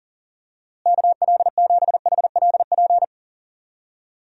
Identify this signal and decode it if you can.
Morse code, heard as a keyed tone, about 40 wpm, 700 Hz. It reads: KL7HLP
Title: DXCC-100 Call Signs - 6 Characters - Repeated 3x - 40wpm